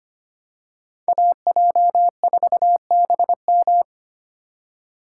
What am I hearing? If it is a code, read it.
Morse code, heard as a keyed tone, about 25 wpm, 700 Hz. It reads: AJ4BM
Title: USA Call Signs - 5 Characters - 3x Character Spacing - 25wpm